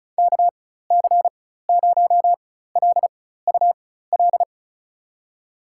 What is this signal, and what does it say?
Morse code, heard as a keyed tone, about 34 wpm, spaced as for 16 wpm, 700 Hz. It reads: KC0LUL
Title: USA Call Signs - 6 Characters - 4x Character Spacing - 35wpm